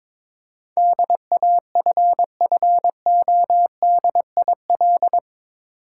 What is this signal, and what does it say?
Morse code, heard as a keyed tone, about 21 wpm, 700 Hz. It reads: DAFFODIL
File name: Difficult Words with Hints - 22wpm